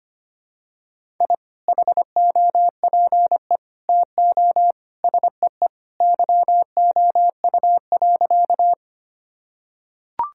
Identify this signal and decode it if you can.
Morse code, heard as a keyed tone, about 25 wpm, 700 Hz. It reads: I HOPE TO SEE YOU.
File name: practice / 25wpm / Sentences with 2-Word Phrases - 25wpm